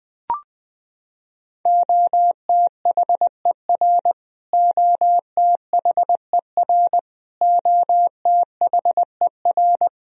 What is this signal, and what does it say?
Morse code, heard as a keyed tone, about 20 wpm, 700 Hz. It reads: OTHER OTHER OTHER
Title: Top 100 Words Repeated 20wpm